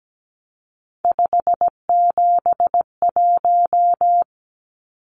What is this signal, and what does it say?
Morse code, heard as a keyed tone, about 17 wpm, 700 Hz. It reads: 571
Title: Numbers - 3-Digit 17wpm